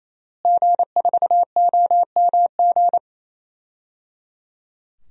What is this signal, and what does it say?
Morse code, heard as a keyed tone, about 28 wpm, 700 Hz. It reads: G4OMG
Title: DXCC-100 Call Signs - 5 Characters - Repeated 3x - 28wpm